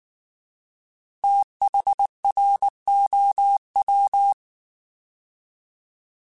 Morse code, heard as a keyed tone, about 19 wpm, 800 Hz. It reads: THROW
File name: Fast TH @18